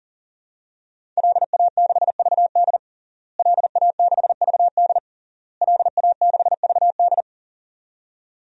Morse code, heard as a keyed tone, about 40 wpm, 700 Hz. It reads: LA6VD LA6VD LA6VD
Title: DXCC-100 Call Signs - 5 Characters - Repeated 3x - 40wpm